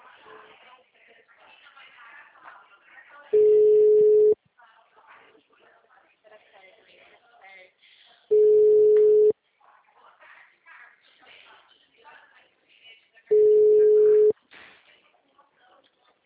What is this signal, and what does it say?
Ringback tone (Europe, 425 Hz), with speech.
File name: out- -2010- - - .76